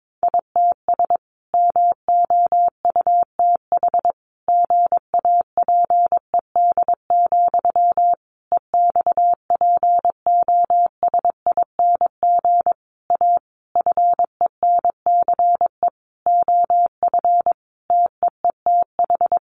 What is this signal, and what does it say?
Morse code, heard as a keyed tone, about 22 wpm, 700 Hz. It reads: ITS MOUTH GAPED, EXPOSING A FENCE OF TEETH
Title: A Sound Of Thunder 22wpm